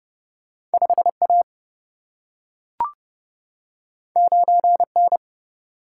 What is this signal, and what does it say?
Morse code, heard as a keyed tone, about 30 wpm, 700 Hz. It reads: 5 A 9 N
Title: Single-Letter Single-Number 30wpm